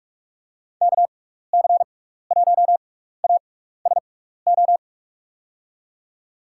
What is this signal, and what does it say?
Morse code, heard as a keyed tone, about 46 wpm, spaced as for 15 wpm, 700 Hz. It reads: KC1ASO
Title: USA Call Signs - 6x Character Spacing - 45wpm